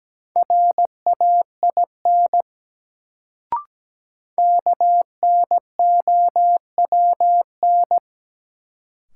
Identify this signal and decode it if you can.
Morse code, heard as a keyed tone, about 17 wpm, 700 Hz. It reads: RAIN KNOWN
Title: Sentences from Top 600 Words - New Word Drill - 17wpm